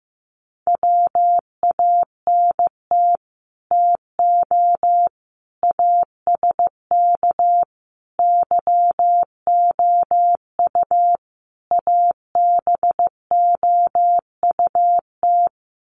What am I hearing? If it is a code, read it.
Morse code, heard as a keyed tone, about 15 wpm, 700 Hz. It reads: WANT TO ASK YOU ABOUT
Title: Sets of 5 Words Encore2 15wpm